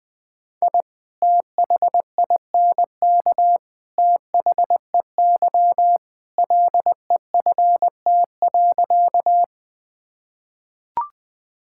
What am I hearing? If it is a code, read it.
Morse code, heard as a keyed tone, about 20 wpm, 700 Hz. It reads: I THINK THEY LEFT.
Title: Sentences with 3-Word Phrases - 20wpm